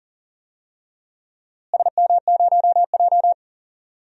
Morse code, heard as a keyed tone, about 40 wpm, 700 Hz. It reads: SM0J
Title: DXCC-100 Call Signs - 4 Characters - Repeated 3x - 40wpm